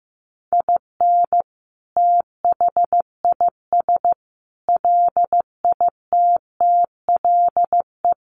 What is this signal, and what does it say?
Morse code, heard as a keyed tone, about 15 wpm, 700 Hz. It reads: IN THIS LITTLE
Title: Sentences from Top 1200 Words 15wpm